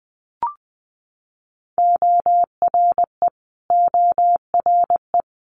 Morse code, heard as a keyed tone, about 20 wpm, 700 Hz. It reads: ORE ORE
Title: Multi-letter Phonemes Repeated 3x 20wpm